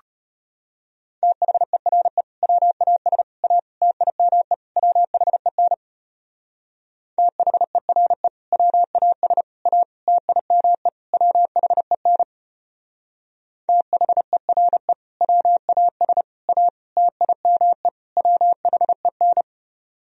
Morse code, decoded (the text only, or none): THERE WAS A TIME WHEN THERE WAS A TIME WHEN THERE WAS A TIME WHEN